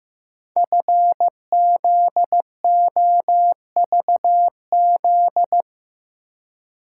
Morse code, heard as a keyed tone, about 15 wpm, 700 Hz. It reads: FZOVZ
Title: Code Groups - 5 Letters - 15wpm